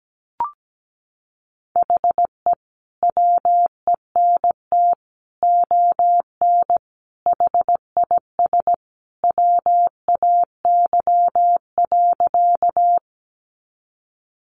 Morse code, heard as a keyed tone, about 17 wpm, 700 Hz. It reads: HE WENT ON HIS WAY.
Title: Sentences from Top 200 Words 17wpm